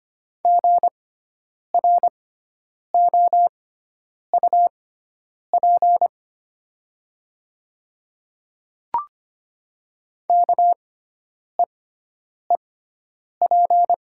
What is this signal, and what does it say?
Morse code, heard as a keyed tone, about 25 wpm, spaced as for 9 wpm, 700 Hz. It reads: GROUP KEEP
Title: 6x Character Spacing - Top 300 Words - 25wpm